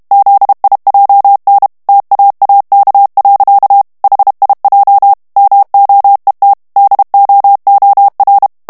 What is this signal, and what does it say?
Morse code, heard as a keyed tone, about 32 wpm, 800 Hz. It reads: ZIJN TAAK. HIJ MOET DOOR
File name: morse mp3 / HOLLANDAIS texte clair 2 vitesse 30 mots minute